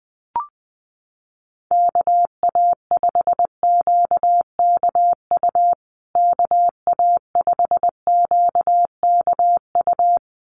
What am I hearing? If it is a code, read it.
Morse code, heard as a keyed tone, about 20 wpm, 700 Hz. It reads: KA5QKU KA5QKU
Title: USA Call Signs Repeated 2x 20wpm